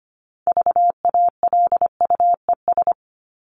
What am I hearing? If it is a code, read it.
Morse code, heard as a keyed tone, about 25 wpm, 700 Hz. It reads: VALUES
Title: ES:S Suffix - Top 100 Words with Base Words 25wpm